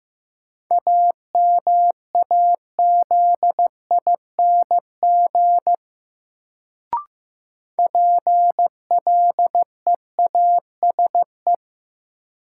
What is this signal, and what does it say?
Morse code, heard as a keyed tone, about 15 wpm, 700 Hz. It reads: AMAZING PLEASE
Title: Sentences from Top 1500 Words - New Word Drill Alt - 15wpm